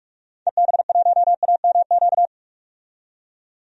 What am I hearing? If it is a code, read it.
Morse code, heard as a keyed tone, about 44 wpm, 700 Hz. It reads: EB1AMQ
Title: DXCC-100 Call Signs - 6 Characters - 45wpm